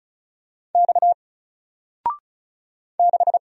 Morse code, heard as a keyed tone, about 36 wpm, 700 Hz. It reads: X 6
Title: Single Letter-Number 35wpm